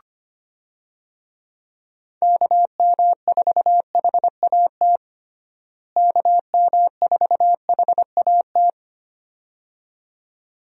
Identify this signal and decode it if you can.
Morse code, heard as a keyed tone, about 25 wpm, 700 Hz. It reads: KM4HAT KM4HAT
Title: USA Call Signs - 6 Characters - Repeated 2x - 25wpm